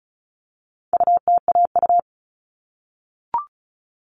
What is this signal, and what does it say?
Morse code, heard as a keyed tone, about 34 wpm, 700 Hz. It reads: UTAU
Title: All Cut Numbers - 4-Digit 35wpm